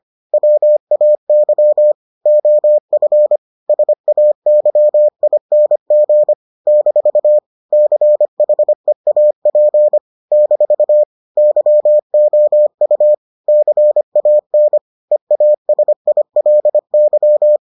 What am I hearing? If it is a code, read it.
Morse code, heard as a keyed tone, about 25 wpm, 600 Hz. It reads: WAY OF SAYING - CHEAP - YOU CAN EASILY